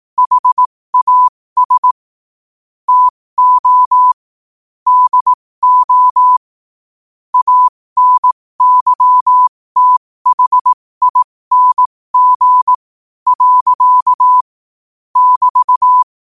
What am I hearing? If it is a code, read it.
Morse code, heard as a keyed tone, about 18 wpm, spaced as for 15 wpm, 1000 Hz. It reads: HAS TO DO ANYTHING. <BT>
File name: qotdfiles / 15wpm